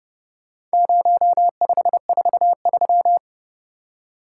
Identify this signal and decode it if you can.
Morse code, heard as a keyed tone, about 28 wpm, 700 Hz. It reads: 0543